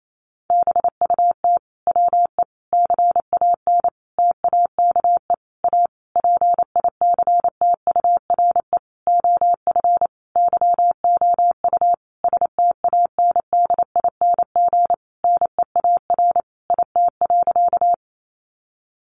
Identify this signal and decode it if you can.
Morse code, heard as a keyed tone, about 28 wpm, 700 Hz. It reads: BUT WE CAN TAKE A PICTURE OF YOU STANDING NEAR IT.
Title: A Sound Of Thunder 28wpm